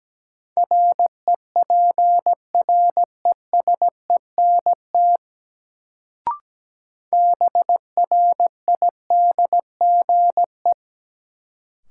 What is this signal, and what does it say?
Morse code, heard as a keyed tone, about 17 wpm, 700 Hz. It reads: REPRESENT BRIDGE
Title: Sentences from Top 1200 Words - New Word Drill - 17wpm